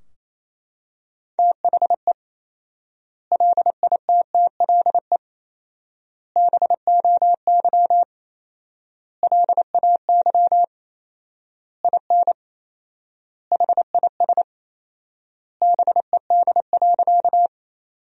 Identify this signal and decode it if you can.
Morse code, heard as a keyed tone, about 28 wpm, 700 Hz. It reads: THE LITTLE BOY LAY IN HIS BED.
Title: 4x Word-Spacing - Sentences from Top 500 Words - 28wpm